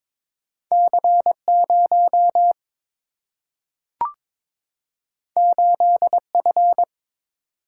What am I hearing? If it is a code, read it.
Morse code, heard as a keyed tone, about 22 wpm, 700 Hz. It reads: C 0 8 F
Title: Single-Letter Single-Number 22wpm